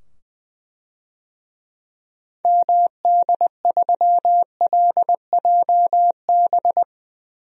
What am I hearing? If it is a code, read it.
Morse code, heard as a keyed tone, about 20 wpm, 700 Hz. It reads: MD3LJB
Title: DXCC-100 Call Signs - 6 Characters - 20wpm